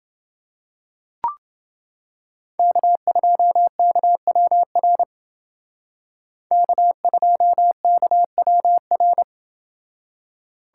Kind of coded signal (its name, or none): Morse code